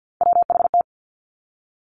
Morse code, heard as a keyed tone, about 50 wpm, 700 Hz. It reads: WHT
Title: Three Challenging Letter-Number Combinations 50wpm